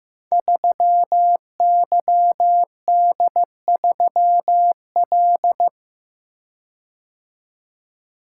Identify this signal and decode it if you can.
Morse code, heard as a keyed tone, about 15 wpm, 700 Hz. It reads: 3YD3L
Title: Code Groups - 5 Letter:Numbers - 15wpm